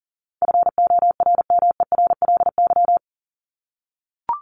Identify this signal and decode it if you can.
Morse code, heard as a keyed tone, about 40 wpm, 700 Hz. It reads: FORMERLY
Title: LY Suffix - Top 100 Words with Base Words 40wpm